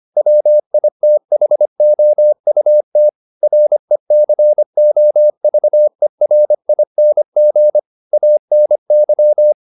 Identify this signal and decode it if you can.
Morse code, heard as a keyed tone, about 25 wpm, 600 Hz. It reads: WITHOUT RECOVERING ANY